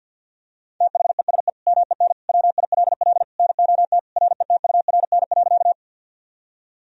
Morse code, heard as a keyed tone, about 52 wpm, 700 Hz. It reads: THESE MEN WILL NOT RETURN.
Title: Sentences from Top 1000 Words - Review Part 1 - 50wpm